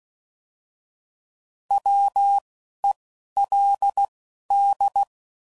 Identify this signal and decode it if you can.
Morse code, heard as a keyed tone, about 16 wpm, spaced as for 11 wpm, 800 Hz. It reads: WELD
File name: Prefixes / By First Letter / Ws